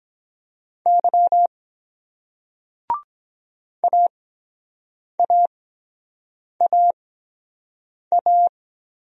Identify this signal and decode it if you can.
Morse code, heard as a keyed tone, about 23 wpm, spaced as for 7 wpm, 700 Hz. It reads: Y AAAA